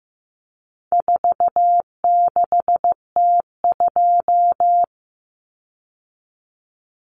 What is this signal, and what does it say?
Morse code, heard as a keyed tone, about 15 wpm, 700 Hz. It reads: 46T2
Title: Common Cut Numbers - 4-Digit 15wpm